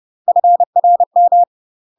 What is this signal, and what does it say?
Morse code, heard as a keyed tone, about 30 wpm, 700 Hz. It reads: FRM
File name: UK MASTER QSOs TRAINING / 30WPM / UK MASTER QSOs 0043